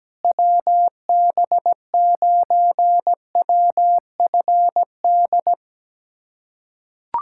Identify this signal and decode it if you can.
Morse code, heard as a keyed tone, about 17 wpm, 700 Hz. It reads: WB9WFD
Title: USA Callsign DE Callsign 17wpm